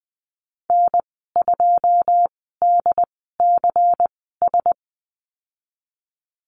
Morse code, heard as a keyed tone, about 20 wpm, spaced as for 15 wpm, 700 Hz. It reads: N2DCS
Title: 122 Lesson 20g - Number 3 - Call Signs 20wpm